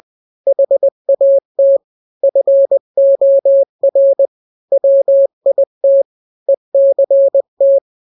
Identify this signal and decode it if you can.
Morse code, heard as a keyed tone, about 20 wpm, 550 Hz. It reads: HAT FOR WIT ECT
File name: n-grams / 3-grams 20 18 WPM